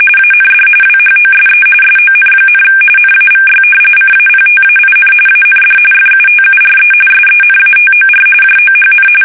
RTTY, about 45.45 baud. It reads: AND WHOSE SON, NAVY CAPTAIN AND FUTURE SENATOR JOHN
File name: fsk850 / battleshipnj